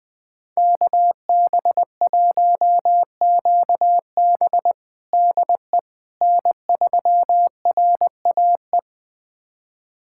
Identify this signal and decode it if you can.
Morse code, heard as a keyed tone, about 20 wpm, 700 Hz. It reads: KB1QB DE N3RAE
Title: USA Callsign DE Callsign - 5 Characters - 20wpm